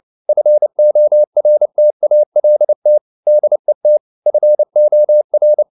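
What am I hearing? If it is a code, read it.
Morse code, heard as a keyed tone, about 29 wpm, 600 Hz. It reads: FORTALT DET FOR
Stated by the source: Fast (QRQ) Morse at 29 wpm, from CW med Gnister.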